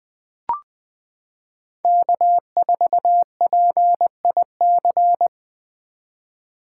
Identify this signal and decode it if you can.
Morse code, heard as a keyed tone, about 20 wpm, 700 Hz. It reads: K4PIC
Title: 2023 SOTA Chaser Call Signs - 20wpm